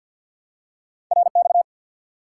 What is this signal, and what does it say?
Morse code, heard as a keyed tone, about 52 wpm, 700 Hz. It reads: RX
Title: Top 90 QSO Elements 50wpm